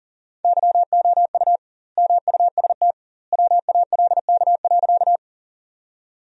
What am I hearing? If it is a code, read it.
Morse code, heard as a keyed tone, about 40 wpm, 700 Hz. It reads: YOU MUST WALK.
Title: 2x Word-Spacing - Sentences from Top 300 Words - 40wpm